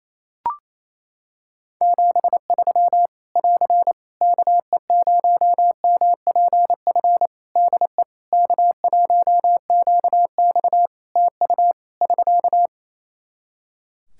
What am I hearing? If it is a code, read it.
Morse code, heard as a keyed tone, about 28 wpm, 700 Hz. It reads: 73 <AR> KE0MPF DE K1QX TU <SK>
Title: Practice QSO - Exchange 4 - Ending - 28wpm